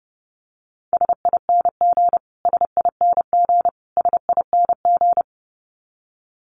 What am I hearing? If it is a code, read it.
Morse code, heard as a keyed tone, about 30 wpm, 700 Hz. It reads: SING SING SING
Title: Top 500 Words Repeated 30wpm